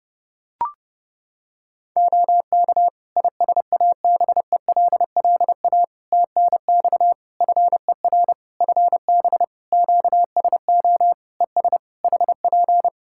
Morse code, heard as a keyed tone, about 30 wpm, 700 Hz. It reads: OK ISABELLA TNX FER FB QSO ES HP
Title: Practice QSO - Exchange 4 - Ending - 30wpm